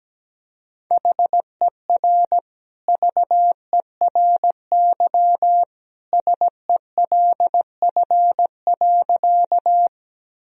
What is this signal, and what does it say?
Morse code, heard as a keyed tone, about 17 wpm, 700 Hz. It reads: HER VERY SELF.
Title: Sentences from Top 1000 Words - Review Part 1 - 17wpm